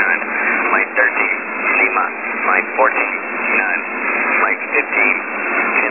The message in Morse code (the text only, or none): none